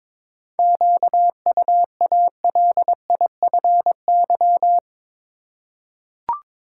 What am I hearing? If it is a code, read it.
Morse code, heard as a keyed tone, about 22 wpm, 700 Hz. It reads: QUALIFY
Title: Seven Letter Words - 22wpm